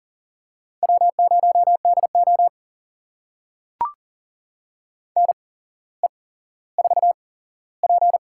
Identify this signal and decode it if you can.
Morse code, heard as a keyed tone, about 40 wpm, spaced as for 17 wpm, 700 Hz. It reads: W0DO NE4P
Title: USA Call Signs - 8x Character Spacing - 40wpm